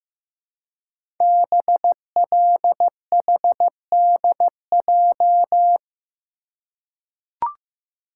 Morse code, heard as a keyed tone, about 15 wpm, 700 Hz. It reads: BLHDJ
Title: Code Groups - 5 Letters - 15wpm